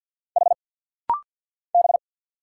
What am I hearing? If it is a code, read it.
Morse code, heard as a keyed tone, about 50 wpm, 690 Hz. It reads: H B
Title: Single Letters - Rapid-Fire 50wpm